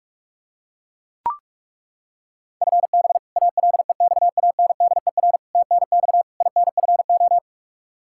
Morse code, heard as a keyed tone, about 41 wpm, 700 Hz. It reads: FB ALEXANDER TNX INFO
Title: K1USN SST QSO Practice 45wpm